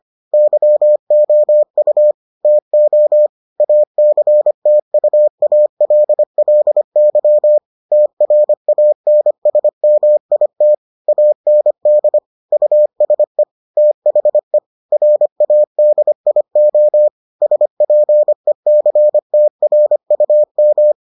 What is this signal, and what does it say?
Morse code, heard as a keyed tone, about 25 wpm, 600 Hz. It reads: YOU TO ACTUALLY TRANSMIT AND USE THE RADIO SPECTRUM